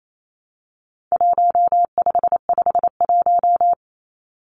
Morse code, heard as a keyed tone, about 28 wpm, 700 Hz. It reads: 1551
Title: Numbers - Contest Style - 4-Digit 28wpm